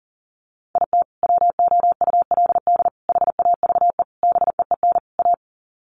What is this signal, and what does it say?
Morse code, heard as a keyed tone, about 40 wpm, 700 Hz. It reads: IT WOULD HAVE BEEN A
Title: Sets of 5 Words Encore 40wpm